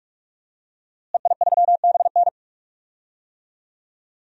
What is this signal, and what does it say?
Morse code, heard as a keyed tone, about 42 wpm, 700 Hz. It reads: EI3BN
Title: DXCC-100 Call Signs - 5 Characters - Repeated 3x - 45wpm